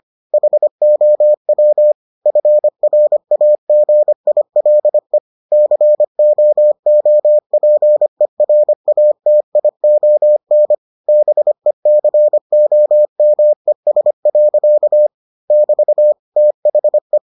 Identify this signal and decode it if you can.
Morse code, heard as a keyed tone, about 25 wpm, 600 Hz. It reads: HOW FRAGILE COOPERATION BECOMES. <BT> THE